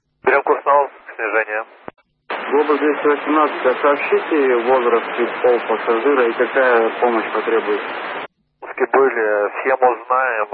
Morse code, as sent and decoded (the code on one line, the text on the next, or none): none